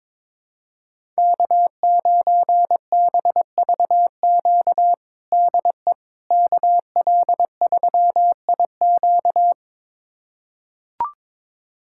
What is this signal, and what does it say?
Morse code, heard as a keyed tone, about 22 wpm, 700 Hz. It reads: K9BVQ DE KL3IQ
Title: USA Callsign DE Callsign - 5 Characters - 22wpm